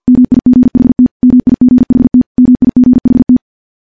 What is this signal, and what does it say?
Morse code, heard as a keyed tone, about 51 wpm, 250 Hz. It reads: MIGHT MIGHT MIGHT
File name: morse / morse-training-50WPM